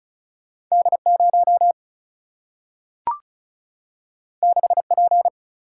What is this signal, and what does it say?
Morse code, heard as a keyed tone, about 36 wpm, 700 Hz. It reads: D0 6P
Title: Single-Letter Single-Number 35wpm